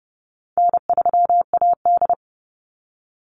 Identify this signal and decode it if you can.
Morse code, heard as a keyed tone, about 30 wpm, 700 Hz. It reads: N3AD
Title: Easy Call Signs - 30wpm